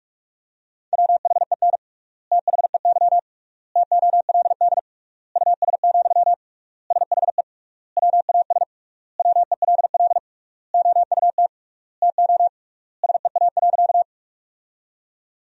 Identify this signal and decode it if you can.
Morse code, heard as a keyed tone, about 42 wpm, 700 Hz. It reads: WHEN THEY TOLD US, SHE WAS WELL OUT TO SEA.
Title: 3x Word-Spacing - Sentences from Top 400 Words - 45wpm